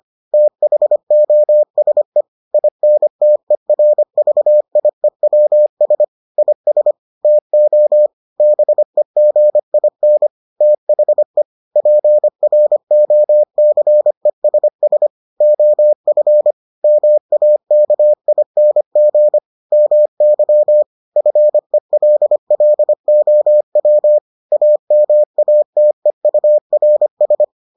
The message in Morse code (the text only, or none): THOSE INTERVIEWS IS TO BEGIN THE PROCESS OF MAKING MY FELLOW AMATEURS